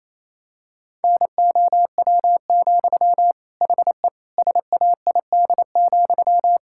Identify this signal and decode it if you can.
Morse code, heard as a keyed tone, about 28 wpm, 700 Hz. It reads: NOW, HE SAID,
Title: Sentences from Top 1600 Words 28wpm